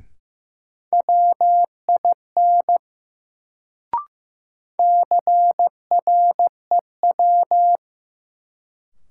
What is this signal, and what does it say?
Morse code, heard as a keyed tone, about 15 wpm, 700 Hz. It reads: WIN CREW
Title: Sentences from Top 1500 Words - New Word Drill Alt - 15wpm